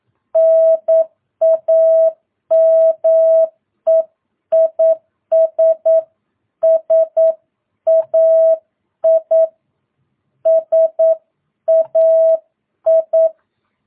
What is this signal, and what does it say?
Morse code, heard as a keyed tone, about 9 wpm, 650 Hz. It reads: NAME IS SAI SAI
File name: jh8cbh / 2022cw